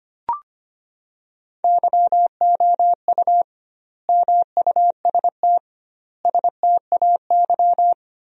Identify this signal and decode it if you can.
Morse code, heard as a keyed tone, about 25 wpm, 700 Hz. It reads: YOU MUST STAY
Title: 2x Word-Spacing - Sentences from Top 500 Words - 25wpm